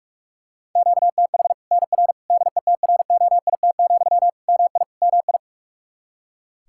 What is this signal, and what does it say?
Morse code, heard as a keyed tone, about 46 wpm, 700 Hz. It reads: QTH NR DETROIT, MI MI
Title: Practice QSO - Exchange 1 - Beginning - 45wpm